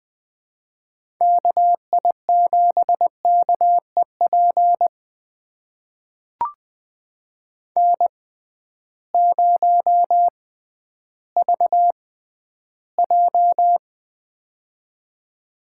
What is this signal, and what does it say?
Morse code, heard as a keyed tone, about 20 wpm, spaced as for 12 wpm, 700 Hz. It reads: KI7KEP N0VJ
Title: USA Call Signs - 6x Character Spacing - 20wpm